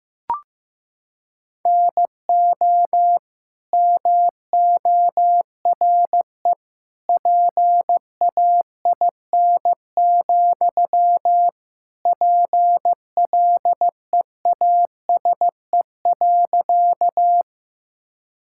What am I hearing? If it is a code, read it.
Morse code, heard as a keyed tone, about 15 wpm, 700 Hz. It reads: NO MORE PAIN, PLEASE.
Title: Sentences from Top 1600 Words 15wpm